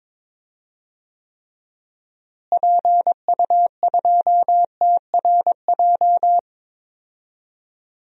Morse code, heard as a keyed tone, about 22 wpm, 700 Hz. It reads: PU2TRJ
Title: DXCC-100 Call Signs - 6 Characters - Repeated 3x - 22wpm